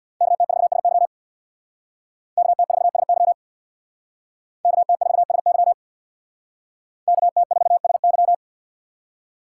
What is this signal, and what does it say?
Morse code, heard as a keyed tone, about 66 wpm, 700 Hz. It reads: KT4SY KT4SY KT4SY KT4SY